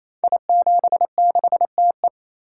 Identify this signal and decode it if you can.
Morse code, heard as a keyed tone, about 28 wpm, 700 Hz. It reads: I76TE
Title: Code Groups - 5 Letter:Numbers - 28wpm